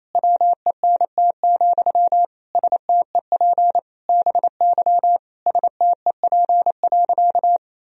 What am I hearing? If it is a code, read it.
Morse code, heard as a keyed tone, about 28 wpm, 700 Hz. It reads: WENT, STEP BY STEP.
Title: Sentences from Top 1000 Words - Review Part 2 - 28wpm